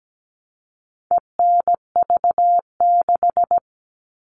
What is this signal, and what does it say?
Morse code, heard as a keyed tone, about 17 wpm, 700 Hz. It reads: ENV6
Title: All Cut Numbers - 4-Digit 17wpm